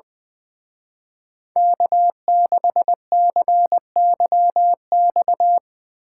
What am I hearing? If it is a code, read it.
Morse code, heard as a keyed tone, about 23 wpm, 700 Hz. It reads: K6CYX
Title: Speed Racing - USA 1x3 Call Signs 20wpm